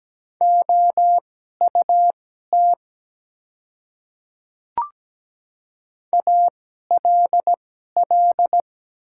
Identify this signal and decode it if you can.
Morse code, heard as a keyed tone, about 17 wpm, spaced as for 12 wpm, 700 Hz. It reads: OUT ALL
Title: 2x Character Spacing - Top 100 Words - 17wpm